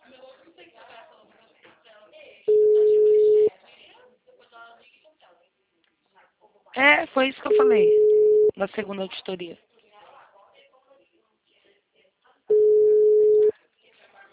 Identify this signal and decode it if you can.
Ringback tone (Europe, 425 Hz), with speech.